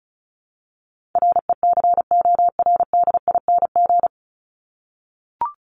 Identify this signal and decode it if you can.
Morse code, heard as a keyed tone, about 34 wpm, 700 Hz. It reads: RECORDING
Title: Top 1501-1750 Words 35wpm